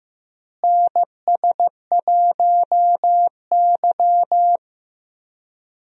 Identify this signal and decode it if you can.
Morse code, heard as a keyed tone, about 15 wpm, 700 Hz. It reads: NS1Y
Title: USA Call Signs - 4 Characters - Repeated 3x - 15wpm